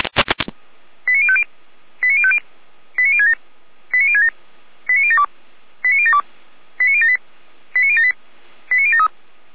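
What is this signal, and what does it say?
Five-tone selective call (ZVEI), heard as quick runs of five tones, tones of about 70 ms. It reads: 89050 89050 89067 89067 89072 89072 89078 89078 89073
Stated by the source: ZVEI five-tone selective call 89014, received with an SDR.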